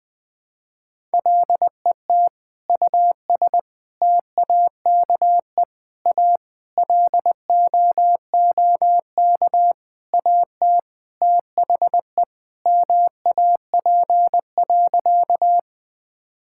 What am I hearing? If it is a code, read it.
Morse code, heard as a keyed tone, about 20 wpm, 700 Hz. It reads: LET US TAKE A LOOK AT THE MAP.
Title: Sentences with 5-Word Phrases - Spoken Beginning - 20wpm